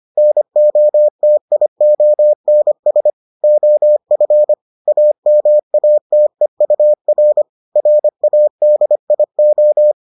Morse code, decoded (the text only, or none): NOTIONS OF AMATEUR RADIO